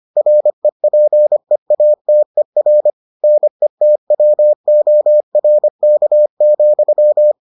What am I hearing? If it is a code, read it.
Morse code, heard as a keyed tone, about 25 wpm, 600 Hz. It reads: REPEATER NETWORK,